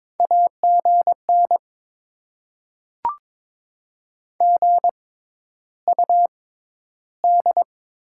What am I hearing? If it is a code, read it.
Morse code, heard as a keyed tone, about 22 wpm, spaced as for 11 wpm, 700 Hz. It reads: AGN GUD
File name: Top 20 QSO Elements - 6x Character Spacing - 22wpm